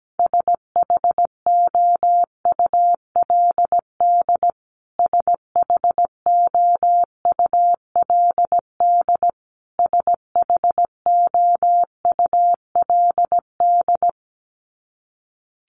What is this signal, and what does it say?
Morse code, heard as a keyed tone, about 17 wpm, 700 Hz. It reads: SHOULD SHOULD SHOULD
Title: Top 500 Words Repeated 17wpm